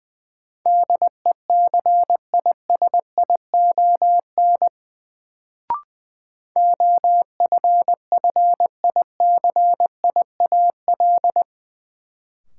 Morse code, heard as a keyed tone, about 21 wpm, 700 Hz. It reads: DECISION OFFICIAL
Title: Sentences from Top 1000 Words - New Word Drill - 20wpm